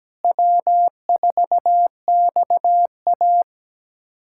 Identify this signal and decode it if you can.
Morse code, heard as a keyed tone, about 17 wpm, 700 Hz. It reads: W4XA
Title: USA Callsign DE Callsign - 4 Characters - 17wpm